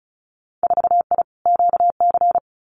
Morse code, heard as a keyed tone, about 36 wpm, 700 Hz. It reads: 4I QC
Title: Winter Field Day Exchanges Encore 35wpm